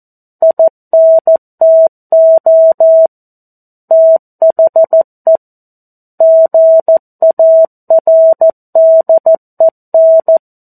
Morse code, heard as a keyed tone, about 14 wpm, 650 Hz. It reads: INTO THE GARDEN
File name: 02-Chapter-02 14 WPM